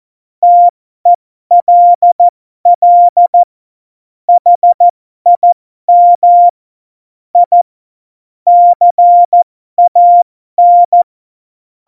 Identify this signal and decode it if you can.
Morse code, heard as a keyed tone, about 14 wpm, spaced as for 12 wpm, 700 Hz. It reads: TELL HIM I CAN
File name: Morsekurs / Witze